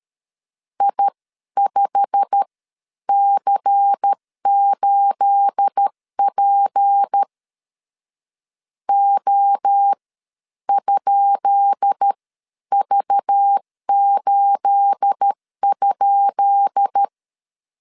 Morse code, heard as a keyed tone, about 13 wpm, spaced as for 10 wpm, 800 Hz. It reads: I5C8P O?V8?
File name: LEZIONI / 10 SIGNS / 10B INT 12wpm 2414